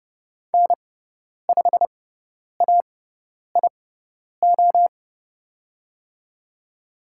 Morse code, heard as a keyed tone, about 30 wpm, spaced as for 10 wpm, 700 Hz. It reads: N5AIO
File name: Lesson 16g - Letter c - Call Signs 30wpm